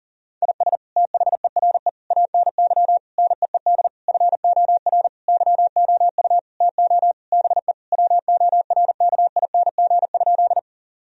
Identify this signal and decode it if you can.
Morse code, heard as a keyed tone, about 40 wpm, 700 Hz. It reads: IS THERE ANY NEED FOR YOU TO BE WORKING?
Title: Sentences from Top 1000 Words - Review Part 4 - 40wpm